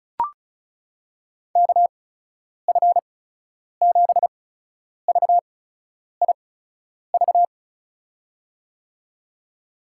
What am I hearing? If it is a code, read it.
Morse code, heard as a keyed tone, about 36 wpm, spaced as for 10 wpm, 700 Hz. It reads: KF7VIV
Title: USA Call Signs - 6 Characters - 8x Character Spacing - 35wpm